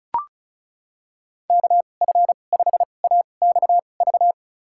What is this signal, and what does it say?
Morse code, heard as a keyed tone, about 36 wpm, spaced as for 25 wpm, 700 Hz. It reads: KF5AXV
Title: USA Call Signs - 6 Characters - 2x Character Spacing - 35wpm